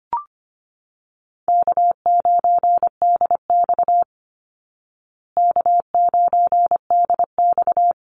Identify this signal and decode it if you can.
Morse code, heard as a keyed tone, about 25 wpm, 700 Hz. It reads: K9DX K9DX
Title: Call Signs Repeated 3x - 25wpm